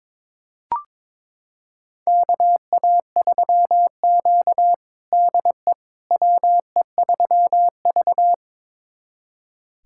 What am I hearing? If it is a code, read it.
Morse code, heard as a keyed tone, about 22 wpm, 700 Hz. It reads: KA3Q DE WE3V
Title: USA Callsign DE Callsign - 4 Characters - 22wpm